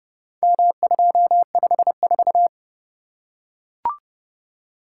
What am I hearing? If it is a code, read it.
Morse code, heard as a keyed tone, about 29 wpm, 700 Hz. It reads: M254